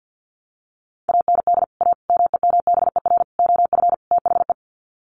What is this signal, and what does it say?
Morse code, heard as a keyed tone, about 50 wpm, 700 Hz. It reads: AND A MEMBER OF THE
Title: Sets of 5 Words 50wpm